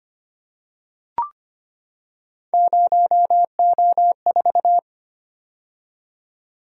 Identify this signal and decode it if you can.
Morse code, heard as a keyed tone, about 25 wpm, 700 Hz. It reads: 0O4
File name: Three Letter-Number Combinations - 25wpm